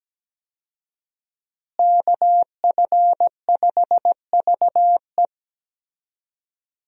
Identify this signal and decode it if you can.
Morse code, heard as a keyed tone, about 17 wpm, 700 Hz. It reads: KF5VE
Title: USA Call Signs - 5 Characters - Repeated 2x - 17wpm